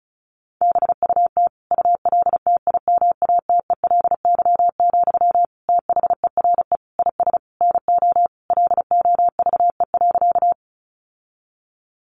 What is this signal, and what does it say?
Morse code, heard as a keyed tone, about 34 wpm, 700 Hz. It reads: BUT ULTIMATELY, THERE IS NO LOVE.
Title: Sentences from Top 1200 Words 35wpm